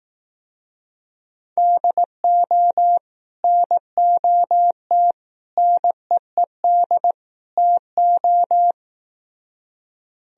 Morse code, heard as a keyed tone, about 18 wpm, 700 Hz. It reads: DO NOT NEED TO
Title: Speed-Racing - Sets of 4 Words 15wpm